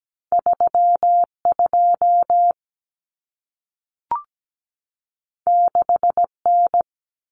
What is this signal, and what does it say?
Morse code, heard as a keyed tone, about 17 wpm, 700 Hz. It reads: 32 6N
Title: Common Cut Numbers - 2-Digit 17wpm